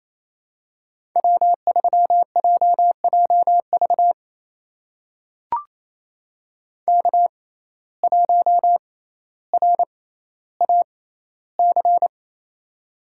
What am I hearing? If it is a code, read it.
Morse code, heard as a keyed tone, about 28 wpm, spaced as for 14 wpm, 700 Hz. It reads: W3JJV K1RAC
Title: USA Call Signs - 5 Characters - 6x Character Spacing - 28wpm